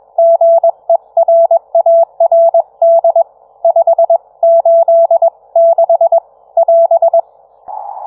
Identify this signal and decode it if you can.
Morse code, heard as a keyed tone, about 21 wpm, 670 Hz. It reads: GERARD 5 8 6 <AS>